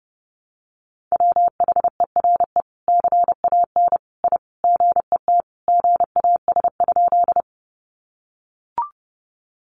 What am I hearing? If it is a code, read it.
Morse code, heard as a keyed tone, about 30 wpm, 700 Hz. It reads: WHERE CAN I GET GAS?
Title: Sentences from Top 1100 Words 30wpm